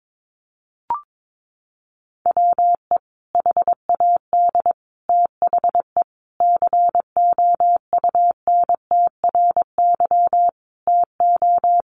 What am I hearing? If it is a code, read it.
Morse code, heard as a keyed tone, about 22 wpm, 700 Hz. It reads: WE HAD THE COUNTRY TO
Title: Sentences from Top 1100 Words 22wpm